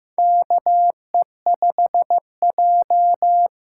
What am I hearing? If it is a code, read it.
Morse code, heard as a keyed tone, about 15 wpm, 700 Hz. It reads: KE5J
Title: USA Call Signs - 4 Characters - 8x Character Spacing - 15wpm